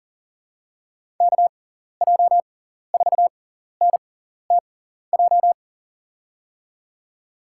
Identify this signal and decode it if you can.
Morse code, heard as a keyed tone, about 40 wpm, spaced as for 14 wpm, 700 Hz. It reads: KJ4NTJ
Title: USA Call Signs - 6 Characters - 6x Character Spacing - 40wpm